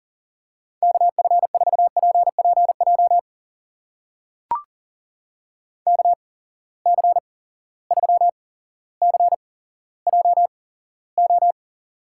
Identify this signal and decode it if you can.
Morse code, heard as a keyed tone, about 40 wpm, spaced as for 17 wpm, 700 Hz. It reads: KF4PPJ KC3CJO
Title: USA Call Signs - 6 Characters - 8x Character Spacing - 40wpm